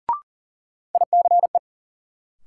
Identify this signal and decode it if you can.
Morse code, heard as a keyed tone, about 40 wpm, 700 Hz. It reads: ICE
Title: Sentences from Top 1000 Words - New Word Drill - 40wpm